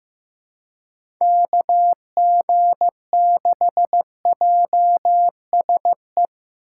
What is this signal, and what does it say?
Morse code, heard as a keyed tone, about 15 wpm, 700 Hz. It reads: KG6JSE
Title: USA Call Signs - 6 Characters - 2x Character Spacing - 15wpm